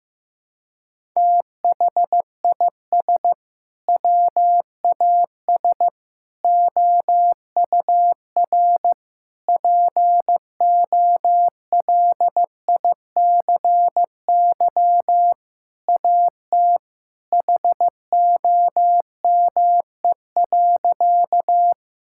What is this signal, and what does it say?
Morse code, heard as a keyed tone, about 16 wpm, 700 Hz. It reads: THIS WAS OUR POLICY AT HOME.
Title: Sentences from Top 1000 Words - Review Part 5 - 15wpm